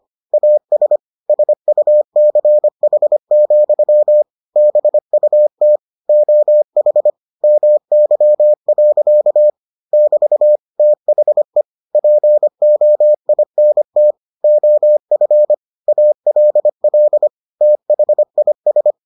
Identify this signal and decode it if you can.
Morse code, heard as a keyed tone, about 25 wpm, 600 Hz. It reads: AS SUCH, BUT OH MY. <BT> THE POINT OF ALL THIS